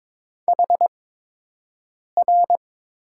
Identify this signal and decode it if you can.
Morse code, heard as a keyed tone, about 22 wpm, spaced as for 6 wpm, 700 Hz. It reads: HR
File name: Top 90 QSO Elements - 8x Character Spacing - 22wpm